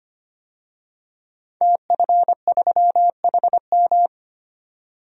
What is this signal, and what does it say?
Morse code, heard as a keyed tone, about 25 wpm, 700 Hz. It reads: TF3HM
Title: DXCC-100 Call Signs - 5 Characters - 25wpm